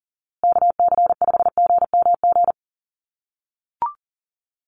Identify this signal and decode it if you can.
Morse code, heard as a keyed tone, about 40 wpm, 700 Hz. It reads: KC5GMG
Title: USA Call Signs Repeated 3x 40wpm